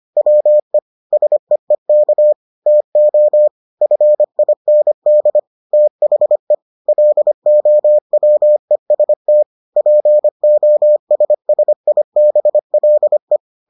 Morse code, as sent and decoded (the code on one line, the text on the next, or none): .-- . / ... . -.- / - --- / ..-. .. -. -.. / - .... / .-.. --- .-- . ... - / .--. --- ... ... .. -... .-.. .
WE SEEK TO FIND THE LOWEST POSSIBLE